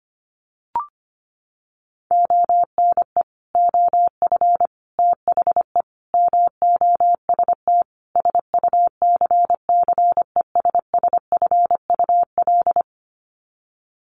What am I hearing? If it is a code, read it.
Morse code, heard as a keyed tone, about 25 wpm, 700 Hz. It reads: ONE OF THE MOST SUCCESSFUL
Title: Sets of 5 Words Encore2 25wpm